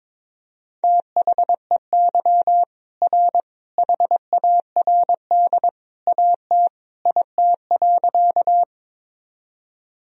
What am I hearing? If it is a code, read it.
Morse code, heard as a keyed tone, about 22 wpm, 700 Hz. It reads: THEY R HARD AT IT.
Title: Sentences from Top 300 Words and Abbreviations 22wpm